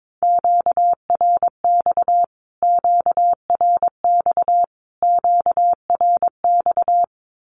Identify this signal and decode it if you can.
Morse code, heard as a keyed tone, about 22 wpm, 700 Hz. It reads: QRX QRX QRX
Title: Q-Codes Repeated 3x 22wpm